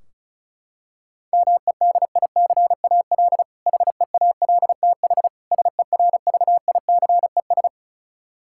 Morse code, heard as a keyed tone, about 36 wpm, 700 Hz. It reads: MEDICAL HEALTH SERVICES
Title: K1USN SST QSO Practice 35wpm